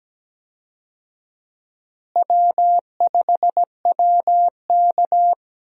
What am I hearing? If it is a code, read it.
Morse code, heard as a keyed tone, about 17 wpm, 700 Hz. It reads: W5WK
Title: USA Call Signs - 4 Characters - Repeated 2x - 17wpm